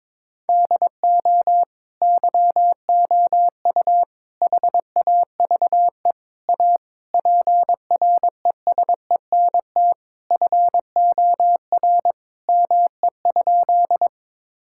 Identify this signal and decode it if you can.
Morse code, heard as a keyed tone, about 23 wpm, 700 Hz. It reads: DO YOU HAVE A PRESENT FOR ME?
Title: Sentences from Top 1000 Words - Review Part 5 - 22wpm